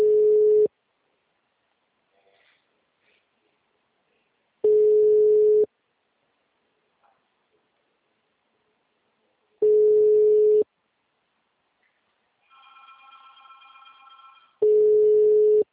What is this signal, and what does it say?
Ringback tone (Europe, 425 Hz).